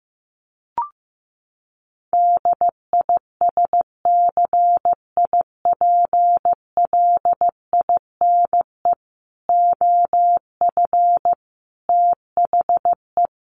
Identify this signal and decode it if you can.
Morse code, heard as a keyed tone, about 15 wpm, 700 Hz. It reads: DISCIPLINE OF THE
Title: Sentences from Top 900 Words 15wpm